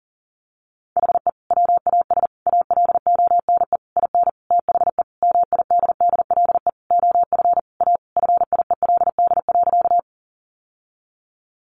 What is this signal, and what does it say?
Morse code, heard as a keyed tone, about 40 wpm, 700 Hz. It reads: HE WAS ALONE IN THE MIDDLE OF A FIELD.
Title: Sentences from Top 1000 Words 40wpm